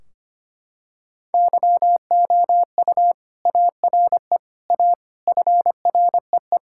Morse code, heard as a keyed tone, about 25 wpm, 700 Hz. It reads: YOU ARE A FREE
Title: Sentences from Top 1600 Words 25wpm